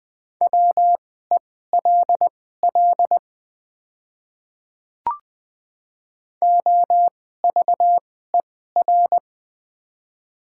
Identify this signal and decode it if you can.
Morse code, heard as a keyed tone, about 20 wpm, spaced as for 15 wpm, 690 Hz. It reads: WELL OVER
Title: 2x Character Spacing - Top 200 Words - 20wpm